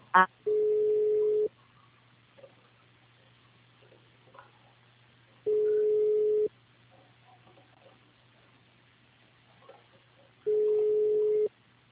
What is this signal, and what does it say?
Ringback tone (Europe, 425 Hz).